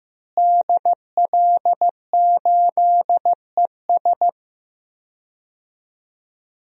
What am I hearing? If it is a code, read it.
Morse code, heard as a keyed tone, about 15 wpm, 700 Hz. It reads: DL8ES
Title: DXCC-100 Call Signs - 5 Characters - Repeated 3x - 15wpm